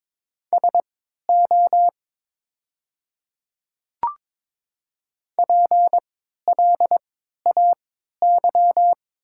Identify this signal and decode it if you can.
Morse code, heard as a keyed tone, about 22 wpm, spaced as for 13 wpm, 700 Hz. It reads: SO PLAY
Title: 3x Character Spacing - Top 200 Words - 22wpm